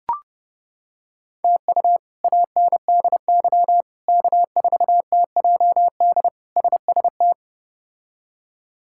Morse code, heard as a keyed tone, about 30 wpm, 700 Hz. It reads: TU ANDY K4TJD SST
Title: K1USN SST QSO Practice - 30wpm